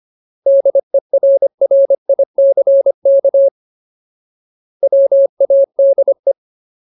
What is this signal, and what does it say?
Morse code, heard as a keyed tone, about 25 wpm, 550 Hz. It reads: DERRICK WADE
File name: Ultimate Word List / 25WPM Ultimate